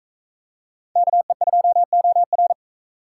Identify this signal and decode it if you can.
Morse code, heard as a keyed tone, about 41 wpm, 700 Hz. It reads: KE2OR